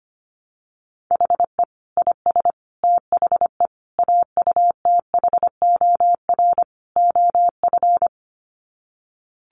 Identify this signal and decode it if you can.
Morse code, heard as a keyed tone, about 25 wpm, 700 Hz. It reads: HE IS THE AUTHOR OF
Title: Sets of 5 Words Encore2 25wpm